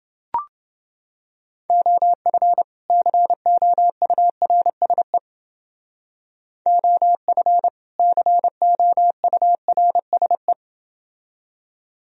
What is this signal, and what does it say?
Morse code, heard as a keyed tone, about 29 wpm, 700 Hz. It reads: OF COURSE OF COURSE